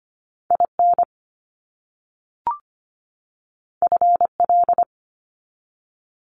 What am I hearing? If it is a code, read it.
Morse code, heard as a keyed tone, about 25 wpm, 700 Hz. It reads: IN FL
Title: US States 25wpm